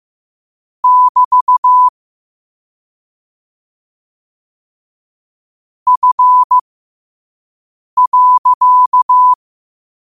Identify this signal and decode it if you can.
Morse code, heard as a keyed tone, about 15 wpm, spaced as for 5 wpm, 1000 Hz. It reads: <BT> F.